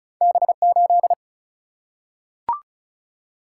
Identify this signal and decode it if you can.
Morse code, heard as a keyed tone, about 34 wpm, 700 Hz. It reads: B8
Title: Single-Letter Single-Number 35wpm